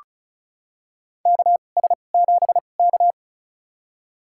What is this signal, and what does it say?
Morse code, heard as a keyed tone, about 36 wpm, spaced as for 25 wpm, 700 Hz. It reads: KS7K
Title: USA Call Signs - 4 Characters - 2x Character Spacing - 35wpm